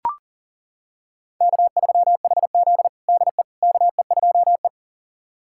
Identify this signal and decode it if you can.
Morse code, heard as a keyed tone, about 37 wpm, 700 Hz. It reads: K3HZ DE KE2E
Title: USA Callsign DE Callsign - 4 Characters - 40wpm